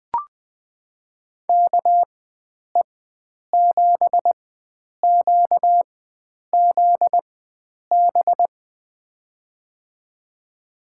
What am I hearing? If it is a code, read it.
Morse code, heard as a keyed tone, about 20 wpm, spaced as for 9 wpm, 700 Hz. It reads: KE7QZB
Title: USA Call Signs - 6 Characters - 4x Character Spacing - 20wpm